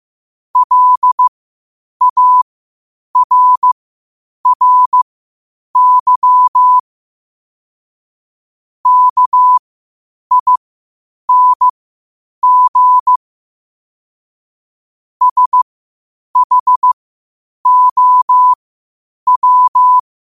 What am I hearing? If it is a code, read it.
Morse code, heard as a keyed tone, about 15 wpm, spaced as for 8 wpm, 1000 Hz. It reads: LARRY KING SHOW
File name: qotdfiles / 8wpm